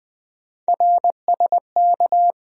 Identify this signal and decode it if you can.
Morse code, heard as a keyed tone, about 20 wpm, 700 Hz. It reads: RSK
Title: USA Call Sign Postfixes - 3 Letters 20wpm